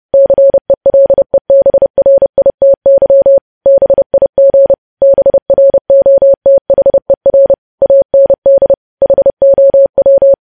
Morse code, heard as a keyed tone, about 30 wpm, 550 Hz. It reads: CELEBRITY BIG BROTHER AND HOW
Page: news headlines in morse code at 30 wpm